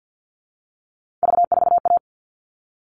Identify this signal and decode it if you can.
Morse code, heard as a keyed tone, about 50 wpm, 700 Hz. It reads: V4A
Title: All Cut Numbers - 3-Digit 50wpm